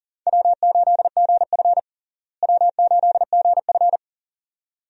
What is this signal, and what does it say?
Morse code, heard as a keyed tone, about 40 wpm, 700 Hz. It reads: W8GF W8GF
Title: USA Call Signs - 4 Characters - Repeated 2x - 40wpm